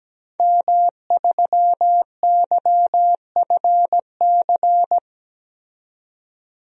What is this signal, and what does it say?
Morse code, heard as a keyed tone, about 17 wpm, 700 Hz. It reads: M3YFC
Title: DXCC-100 Call Signs - 5 Characters - 17wpm